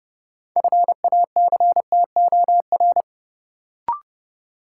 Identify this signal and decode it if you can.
Morse code, heard as a keyed tone, about 29 wpm, 700 Hz. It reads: FACTOR
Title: Sentences from Top 1300 Words - New Word Drill Alt - 30wpm